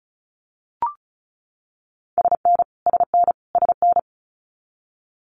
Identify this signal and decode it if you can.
Morse code, heard as a keyed tone, about 34 wpm, 700 Hz. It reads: SN SN SN
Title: Abbreviations Repeated 3x - 35wpm